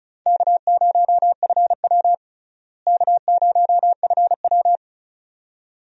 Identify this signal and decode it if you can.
Morse code, heard as a keyed tone, about 36 wpm, 700 Hz. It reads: K0FW K0FW
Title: USA Call Signs - 4 Characters - Repeated 2x - 35wpm